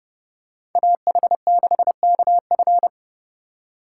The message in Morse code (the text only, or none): AH6KF